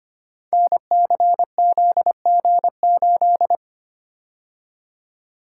Morse code, heard as a keyed tone, about 25 wpm, 700 Hz. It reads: NCZG8
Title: Code Groups - 5 Letter:Numbers - 25wpm